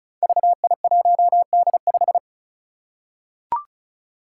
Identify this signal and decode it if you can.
Morse code, heard as a keyed tone, about 36 wpm, 700 Hz. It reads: VI1D5
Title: Five Challenging Letter-Number Combinations 35wpm